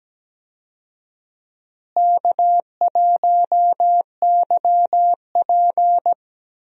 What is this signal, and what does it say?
Morse code, heard as a keyed tone, about 17 wpm, 700 Hz. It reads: K1YP
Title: USA Call Signs - 4 Characters - Repeated 3x - 17wpm